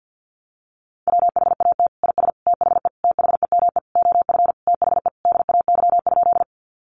Morse code, heard as a keyed tone, about 49 wpm, 700 Hz. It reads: WHAT IS THE THEME OF THE DAY?
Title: Sentences from Top 1100 Words 50wpm